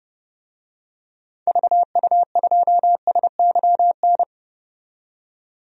Morse code, heard as a keyed tone, about 30 wpm, 700 Hz. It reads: VU2SYN
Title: DXCC-100 Call Signs - 6 Characters - 30wpm